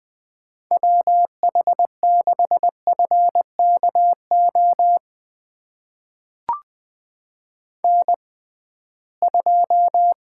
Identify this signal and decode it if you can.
Morse code, heard as a keyed tone, about 21 wpm, spaced as for 15 wpm, 700 Hz. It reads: WH6FKO N2
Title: USA Call Signs - 6x Character Spacing - 20wpm